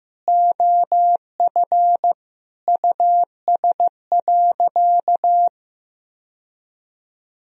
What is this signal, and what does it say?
Morse code, heard as a keyed tone, about 16 wpm, 700 Hz. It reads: OF US.
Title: Sentences from Top 1000 Words - Review Part 2 - 15wpm